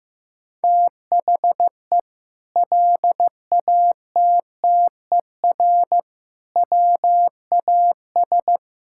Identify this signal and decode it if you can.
Morse code, heard as a keyed tone, about 15 wpm, 700 Hz. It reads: THE LATTER WAS
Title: Sentences from Top 1600 Words 15wpm